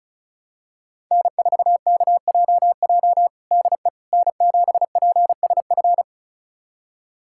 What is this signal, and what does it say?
Morse code, heard as a keyed tone, about 36 wpm, 700 Hz. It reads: N4KJJ DE N7PSF
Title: USA Callsign DE Callsign - 5 Characters - 35wpm